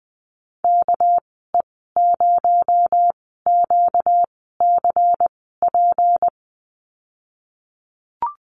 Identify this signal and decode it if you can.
Morse code, heard as a keyed tone, about 20 wpm, spaced as for 15 wpm, 700 Hz. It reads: KE0QCP
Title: 238 Lesson 38g - Letter q - Call Signs 20wpm